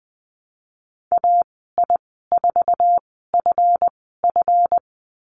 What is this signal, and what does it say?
Morse code, heard as a keyed tone, about 20 wpm, spaced as for 15 wpm, 700 Hz. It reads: AI4FF
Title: 186 Lesson 30g - Letter b - Call Signs 20wpm